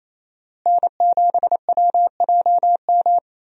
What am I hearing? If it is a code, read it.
Morse code, heard as a keyed tone, about 28 wpm, 700 Hz. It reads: N7WJM
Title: USA Call Signs - 5 Characters - Repeated 3x - 28wpm